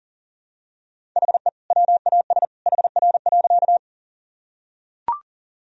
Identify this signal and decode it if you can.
Morse code, heard as a keyed tone, about 40 wpm, 700 Hz. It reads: HE WAS HR.
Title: Sentences from Top 200 Words and Abbreviations 40wpm